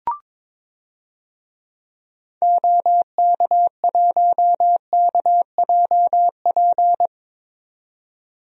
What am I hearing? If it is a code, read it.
Morse code, heard as a keyed tone, about 22 wpm, 700 Hz. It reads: OK1KJP
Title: DXCC-100 Call Signs - 6 Characters - 22wpm